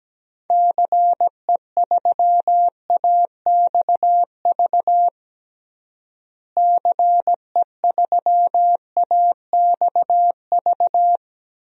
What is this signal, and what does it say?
Morse code, heard as a keyed tone, about 17 wpm, 700 Hz. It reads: CE3AXV CE3AXV
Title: DXCC-100 Call Signs - 6 Characters - Repeated 2x - 17wpm